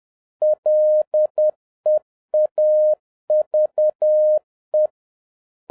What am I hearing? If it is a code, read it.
Morse code, heard as a keyed tone, about 10 wpm, 610 Hz. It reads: LEAVE